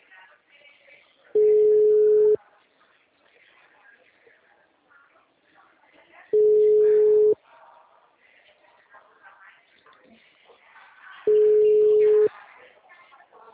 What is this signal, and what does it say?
Ringback tone (Europe, 425 Hz), with speech.
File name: out- -2010- - - .692